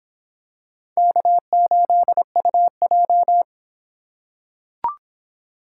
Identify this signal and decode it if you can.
Morse code, heard as a keyed tone, about 26 wpm, 700 Hz. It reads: K8UJ